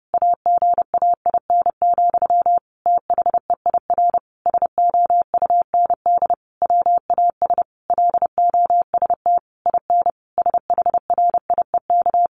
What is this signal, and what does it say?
Morse code, heard as a keyed tone, about 28 wpm, 700 Hz. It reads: AGAIN, THEIR SOUND WAS LOST IN SHRIEK
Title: A Sound Of Thunder 30wpm